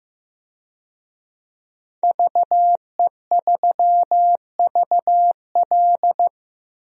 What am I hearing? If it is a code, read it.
Morse code, heard as a keyed tone, about 15 wpm, 700 Hz. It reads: VE3VL
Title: DXCC-100 Call Signs - 5 Characters - Repeated 2x - 15wpm